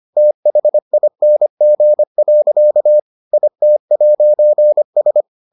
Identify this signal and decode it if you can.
Morse code, heard as a keyed tone, about 25 wpm, 600 Hz. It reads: THING. IT'S